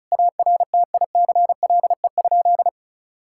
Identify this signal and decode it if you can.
Morse code, heard as a keyed tone, about 36 wpm, 700 Hz. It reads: ARTICLE?
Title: Sentences from Top 1000 Words - Review Part 5 - 35wpm